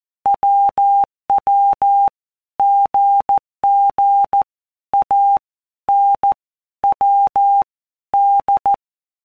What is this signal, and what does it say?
Morse code, heard as a keyed tone, about 14 wpm, 800 Hz. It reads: WW GG A N W D
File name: Basic Letter Groups By Blocks / Groups@13 / AWUNDG@13